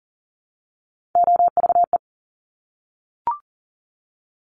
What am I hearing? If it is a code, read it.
Morse code, heard as a keyed tone, about 43 wpm, 700 Hz. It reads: OVE
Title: Top Three Letter Combinations in Words 40wpm